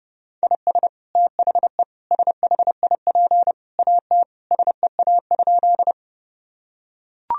Morse code, heard as a keyed tone, about 30 wpm, 700 Hz. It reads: IS THE SHIP AT SEA?
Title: Sentences from Top 1000 Words - Review Part 2 - 30wpm